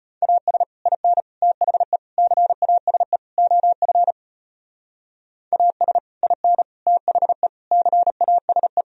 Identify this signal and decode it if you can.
Morse code, heard as a keyed tone, about 30 wpm, 700 Hz. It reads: AS IN THE CASE OF AS IN THE CASE
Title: Speed-Racing - Sets of 5 Words 25wpm